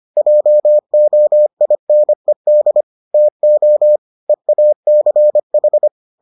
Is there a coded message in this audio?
Yes: Morse code from 0.2 to 5.9 s, heard as a keyed tone, about 25 wpm, 600 Hz. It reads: JOINED TO EACH